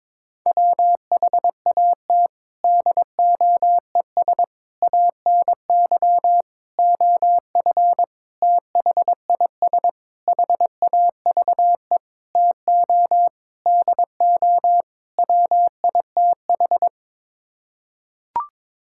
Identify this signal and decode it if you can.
Morse code, heard as a keyed tone, about 22 wpm, 700 Hz. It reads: WHAT DOES ANY OF THIS HAVE TO DO WITH
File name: Sets of 9 Words - 22wpm